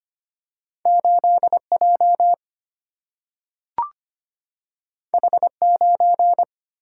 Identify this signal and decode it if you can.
Morse code, heard as a keyed tone, about 25 wpm, 700 Hz. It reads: 8J H9
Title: Single-Letter Single-Number 25wpm